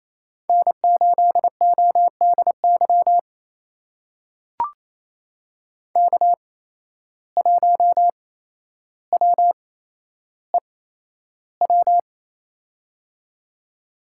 Morse code, heard as a keyed tone, about 28 wpm, spaced as for 12 wpm, 700 Hz. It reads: N8ODY K1WEW
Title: USA Call Signs - 5 Characters - 8x Character Spacing - 28wpm